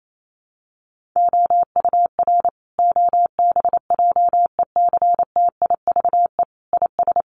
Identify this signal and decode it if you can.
Morse code, heard as a keyed tone, about 28 wpm, 700 Hz. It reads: OUR OBJECTIVE IS
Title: Sentences from Top 1200 Words 28wpm